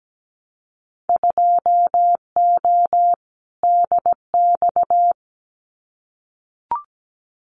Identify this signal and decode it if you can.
Morse code, heard as a keyed tone, about 17 wpm, 700 Hz. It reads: 2O DX
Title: Winter Field Day Exchanges Encore 17wpm